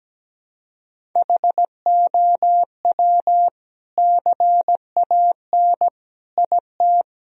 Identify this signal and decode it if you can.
Morse code, heard as a keyed tone, about 17 wpm, 700 Hz. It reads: HOW CAN IT
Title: Sentences from Top 1000 Words - Review Part 2 - 17wpm